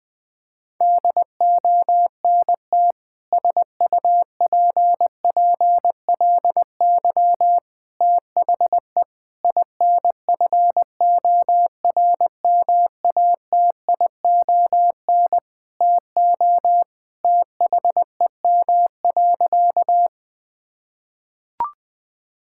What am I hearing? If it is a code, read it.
Morse code, heard as a keyed tone, about 20 wpm, 700 Hz. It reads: DONT SUPPLY THE INFORMATION TO THEM.
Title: Sentences from Top 1000 Words - Review Part 5 - 20wpm